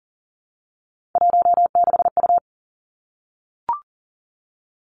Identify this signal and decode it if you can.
Morse code, heard as a keyed tone, about 38 wpm, 700 Hz. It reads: 16U
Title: Three Challenging Letter-Number Combinations 40wpm